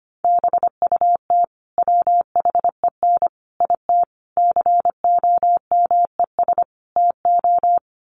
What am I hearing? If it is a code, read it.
Morse code, heard as a keyed tone, about 25 wpm, 700 Hz. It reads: BUT WHEN IT COMES TO
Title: Sets of 5 Words 25wpm